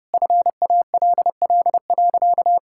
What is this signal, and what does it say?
Morse code, heard as a keyed tone, about 30 wpm, 700 Hz. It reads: FALL.
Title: Sentences from Top 1000 Words - Review Part 6 - 30wpm